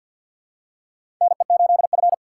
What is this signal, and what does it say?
Morse code, heard as a keyed tone, about 52 wpm, 700 Hz. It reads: NE8F
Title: USA Call Signs - 4 Characters - 2x Character Spacing - 50wpm